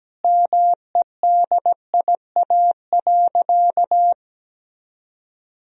Morse code, heard as a keyed tone, about 17 wpm, 700 Hz. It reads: MEDIA.
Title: Sentences from Top 1000 Words - Review Part 4 - 17wpm